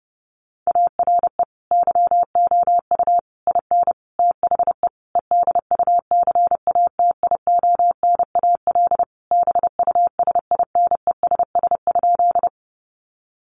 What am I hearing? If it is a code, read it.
Morse code, heard as a keyed tone, about 30 wpm, 700 Hz. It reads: ARE YOU IN THE EDUCATIONAL BUSINESS?
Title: Sentences from Top 1200 Words 30wpm